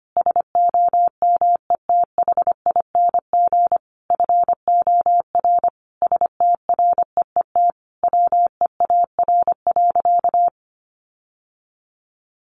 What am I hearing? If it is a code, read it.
Morse code, heard as a keyed tone, about 25 wpm, 700 Hz. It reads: SOMETHING FOR STREET WEAR.
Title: Sentences from Top 1500 Words 25wpm